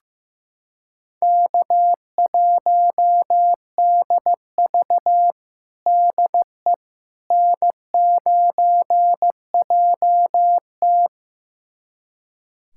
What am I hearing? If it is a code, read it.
Morse code, heard as a keyed tone, about 15 wpm, 700 Hz. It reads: K1DV DE N9JT
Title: USA Callsign DE Callsign - 4 Characters - 15wpm